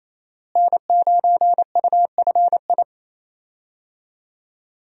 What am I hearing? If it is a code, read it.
Morse code, heard as a keyed tone, about 28 wpm, 700 Hz. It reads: N9UFI
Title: USA Call Signs - 5 Characters - Repeated 3x - 28wpm